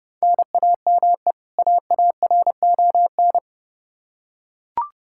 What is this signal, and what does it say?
Morse code, heard as a keyed tone, about 32 wpm, 700 Hz. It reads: NAME AARON
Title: Practice QSO - Exchange 1 - Beginning - 30wpm